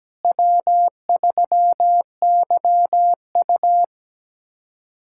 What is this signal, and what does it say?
Morse code, heard as a keyed tone, about 17 wpm, 700 Hz. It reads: W3YU
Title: USA Callsign DE Callsign - 4 Characters - 17wpm